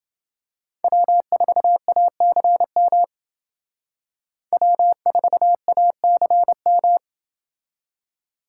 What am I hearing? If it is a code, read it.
Morse code, heard as a keyed tone, about 29 wpm, 700 Hz. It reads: W4ACM W4ACM